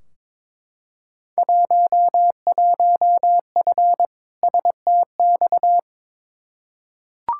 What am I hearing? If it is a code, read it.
Morse code, heard as a keyed tone, about 22 wpm, 700 Hz. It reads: 11F STX
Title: Field Day Exchanges Encore - 22wpm